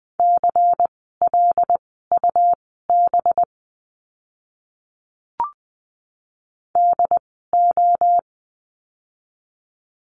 Morse code, heard as a keyed tone, about 20 wpm, spaced as for 14 wpm, 700 Hz. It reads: CLUB DO
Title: 185 Lesson 30f - Letter b - Words 20wpm